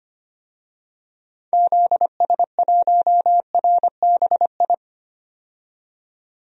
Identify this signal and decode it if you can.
Morse code, heard as a keyed tone, about 25 wpm, 700 Hz. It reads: ZS1RBI
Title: DXCC-100 Call Signs - 6 Characters - 25wpm